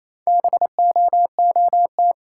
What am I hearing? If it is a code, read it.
Morse code, heard as a keyed tone, about 28 wpm, 700 Hz. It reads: BOOT
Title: Four Letter Words - 28wpm